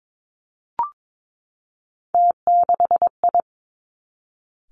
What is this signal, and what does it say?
Morse code, heard as a keyed tone, about 22 wpm, 700 Hz. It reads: T6I
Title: Three Challenging Letter-Number Combinations 22wpm